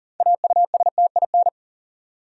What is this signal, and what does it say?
Morse code, heard as a keyed tone, about 40 wpm, 700 Hz. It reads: AUSTIN
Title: USA Common Names - 40wpm